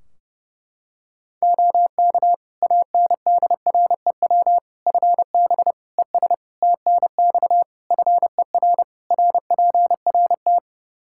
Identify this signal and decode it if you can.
Morse code, heard as a keyed tone, about 31 wpm, 700 Hz. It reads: OK ANDREW FB ES TNX FER RPRT
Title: Practice QSO - Exchange 2 - Info - 30wpm